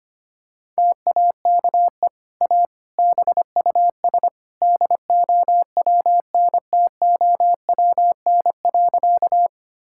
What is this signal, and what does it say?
Morse code, heard as a keyed tone, about 25 wpm, 700 Hz. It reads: TAKE A BUS DOWNTOWN.
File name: Easy Sentences with Hints - 25wpm